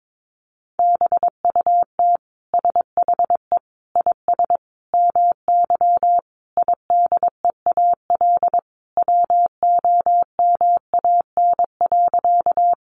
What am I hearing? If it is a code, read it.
Morse code, heard as a keyed tone, about 22 wpm, 700 Hz. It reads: BUT SHE IS MY IDEAL WOMAN.
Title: Sentences from Top 1400 Words 22wpm